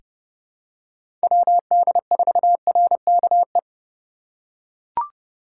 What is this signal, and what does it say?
Morse code, heard as a keyed tone, about 30 wpm, 690 Hz. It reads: WD4RKE
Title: USA Call Signs - 3x Character Spacing - 30wpm